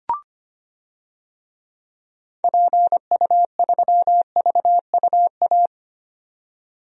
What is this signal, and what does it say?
Morse code, heard as a keyed tone, about 25 wpm, 690 Hz. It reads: PU3VUA
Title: DXCC-100 Call Signs - 6 Characters - 25wpm